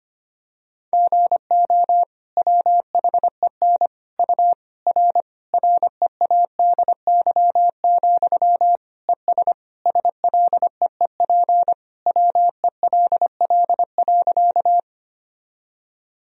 Morse code, heard as a keyed tone, about 25 wpm, 700 Hz. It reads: GO WHEN U R READY, ES SLEEP WELL.
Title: Sentences from Top 500 Words and Abbreviations 25wpm